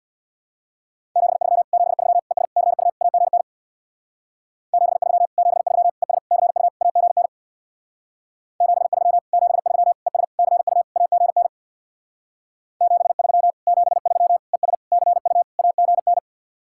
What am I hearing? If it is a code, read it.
Morse code, heard as a keyed tone, about 60 wpm, 700 Hz. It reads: 73 73 ES CU AGN 73 73 ES CU AGN 73 73 ES CU AGN 73 73 ES CU AGN
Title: Speed-Racing Common QSO Phrases Encore - 50wpm